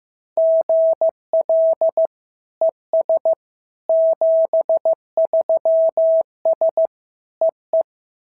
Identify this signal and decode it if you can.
Morse code, heard as a keyed tone, about 15 wpm, 690 Hz. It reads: GL ES 73S EE
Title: SOTA QSO Practice 15wpm